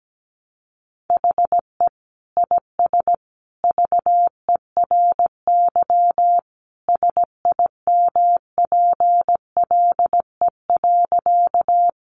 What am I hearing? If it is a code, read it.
Morse code, heard as a keyed tone, about 17 wpm, 700 Hz. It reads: HE IS VERY SIMPLE.
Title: Sentences from Top 400 Words 17wpm